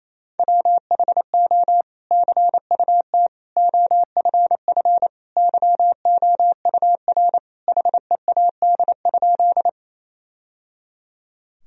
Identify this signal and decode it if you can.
Morse code, heard as a keyed tone, about 28 wpm, 700 Hz. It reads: WHO CUT OFF YOUR HEAD?
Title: Sentences from Top 1000 Words - Review Part 1 - 28wpm